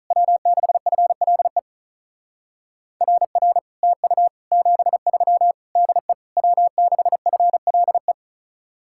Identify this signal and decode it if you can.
Morse code, heard as a keyed tone, about 44 wpm, 700 Hz. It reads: W6FLE RR TU 73 DE W6FLE